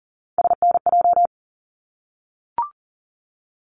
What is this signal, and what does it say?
Morse code, heard as a keyed tone, about 40 wpm, 700 Hz. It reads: SNJ
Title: ARRL Sections 40wpm